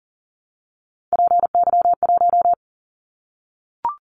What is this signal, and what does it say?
Morse code, heard as a keyed tone, about 38 wpm, 700 Hz. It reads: PY1
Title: Callsign Prefixes for Top 50 DXCC - 40wpm